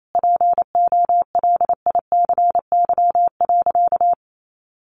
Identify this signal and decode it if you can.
Morse code, heard as a keyed tone, about 28 wpm, 700 Hz. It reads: POLICY.
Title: Sentences from Top 1200 Words 28wpm